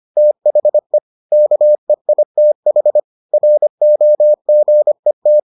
Morse code, heard as a keyed tone, about 25 wpm, 600 Hz. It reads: THE KEITH ROGET